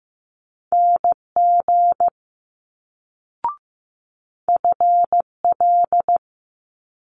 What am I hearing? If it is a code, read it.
Morse code, heard as a keyed tone, about 15 wpm, 700 Hz. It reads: NG FL
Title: Multi-letter Phonemes Quick Fire 15wpm